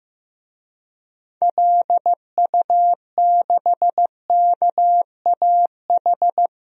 Morse code, heard as a keyed tone, about 16 wpm, 700 Hz. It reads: LU6KAH
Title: DXCC-100 Call Signs - 6 Characters - Repeated 2x - 15wpm